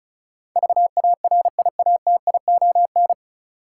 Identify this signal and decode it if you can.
Morse code, heard as a keyed tone, about 36 wpm, 700 Hz. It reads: VARIATION
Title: Sentences from Top 1600 Words - New Word Drill Alt - 35wpm